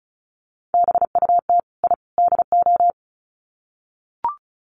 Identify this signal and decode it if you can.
Morse code, heard as a keyed tone, about 34 wpm, 700 Hz. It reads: BUT I DO
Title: Sets of 3 Words 35wpm